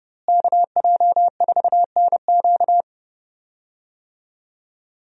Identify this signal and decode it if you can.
Morse code, heard as a keyed tone, about 31 wpm, 700 Hz. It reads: KJ4NQ
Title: USA Call Signs - 5 Characters - Repeated 3x - 30wpm